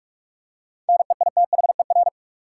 Morse code, heard as a keyed tone, about 43 wpm, 700 Hz. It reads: NEITHER
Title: Seven Letter Words - 45wpm